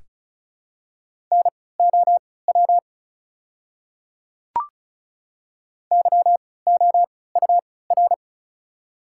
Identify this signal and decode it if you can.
Morse code, heard as a keyed tone, about 36 wpm, spaced as for 20 wpm, 690 Hz. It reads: NOW YOUR
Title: 3x Character Spacing - Top 100 Words - 35wpm